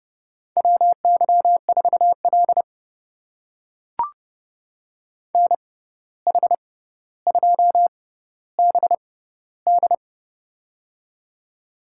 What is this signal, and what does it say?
Morse code, heard as a keyed tone, about 30 wpm, spaced as for 14 wpm, 700 Hz. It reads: WY4L NH2BD
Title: USA Call Signs - 6x Character Spacing - 30wpm